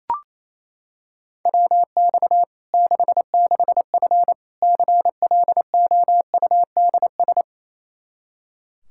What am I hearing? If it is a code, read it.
Morse code, heard as a keyed tone, about 28 wpm, 700 Hz. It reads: WX 66F CLOUDS
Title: Practice QSO - Exchange 2 - Info - 28wpm